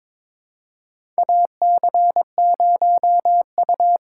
Morse code, heard as a keyed tone, about 21 wpm, 700 Hz. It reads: AC0U
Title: USA Call Signs - 8x Character Spacing - 22wpm